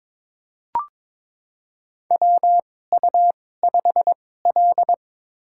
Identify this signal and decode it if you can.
Morse code, heard as a keyed tone, about 22 wpm, spaced as for 16 wpm, 700 Hz. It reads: WU5L
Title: USA Call Signs - 4 Characters - 2x Character Spacing - 22wpm